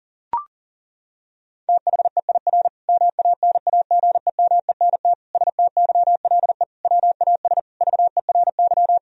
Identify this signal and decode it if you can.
Morse code, heard as a keyed tone, about 40 wpm, 700 Hz. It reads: THEIR MANAGEMENT STYLE WAS VERY
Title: Sentences from Top 1000 Words - Review Part 4 - 40wpm